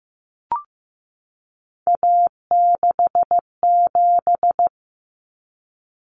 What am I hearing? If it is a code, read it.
Morse code, heard as a keyed tone, about 15 wpm, 700 Hz. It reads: A67
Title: Common Cut Numbers - 3-Digit 15wpm